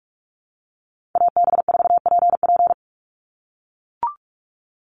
Morse code, heard as a keyed tone, about 44 wpm, 700 Hz. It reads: AB4PP
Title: NA SOTA Chaser Callsigns 45wpm